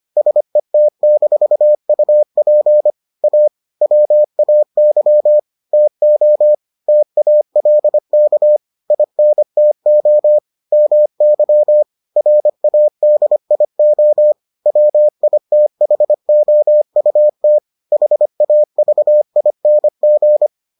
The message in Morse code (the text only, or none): SET-UP A WAY TO TALK INTO MY RADIO WITHOUT HAVING